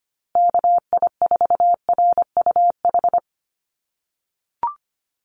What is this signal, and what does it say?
Morse code, heard as a keyed tone, about 25 wpm, 700 Hz. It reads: KI4RUH
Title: USA Call Signs Repeated 2x 25wpm